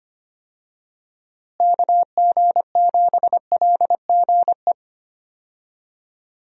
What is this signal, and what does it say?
Morse code, heard as a keyed tone, about 25 wpm, 700 Hz. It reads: KG7LGE
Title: USA Call Signs - 6 Characters - Repeated 3x - 25wpm